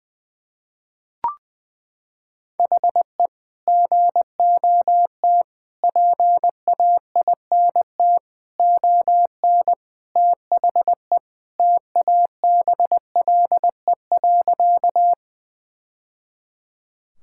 Morse code, heard as a keyed tone, about 20 wpm, 700 Hz. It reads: HE GOT PAINT ON THE TABLE.
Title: Sentences from Top 1000 Words - Review Part 6 - 20wpm